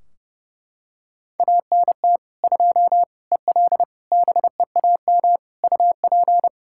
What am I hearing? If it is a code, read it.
Morse code, heard as a keyed tone, about 32 wpm, 700 Hz. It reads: ANT 2 EL BEAM UP
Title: Practice QSO - Exchange 2 - Info - 30wpm